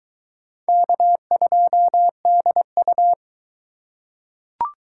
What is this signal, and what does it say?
Morse code, heard as a keyed tone, about 22 wpm, 700 Hz. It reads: K2DU